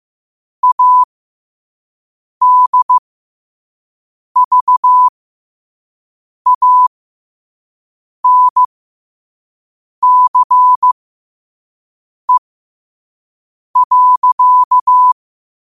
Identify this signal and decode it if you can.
Morse code, heard as a keyed tone, about 15 wpm, spaced as for 5 wpm, 1000 Hz. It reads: ADVANCE.
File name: qotdfiles / 5wpm